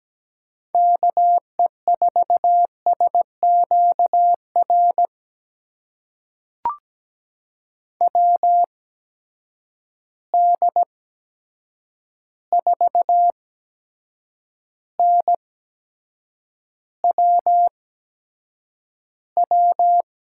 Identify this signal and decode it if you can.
Morse code, heard as a keyed tone, about 17 wpm, spaced as for 7 wpm, 700 Hz. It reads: KE4SQR WD4NWW
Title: USA Call Signs - 6 Characters - 8x Character Spacing - 17wpm